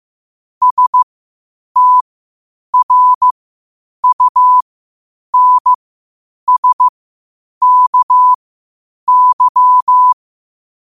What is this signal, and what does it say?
Morse code, heard as a keyed tone, about 15 wpm, spaced as for 8 wpm, 1000 Hz. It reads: STRUNSKY
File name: qotdfiles / 8wpm